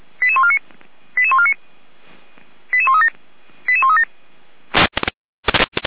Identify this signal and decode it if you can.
Five-tone selective call (ZVEI), heard as quick runs of five tones, tones of about 70 ms. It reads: 88149 88149 88137 88137
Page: ZVEI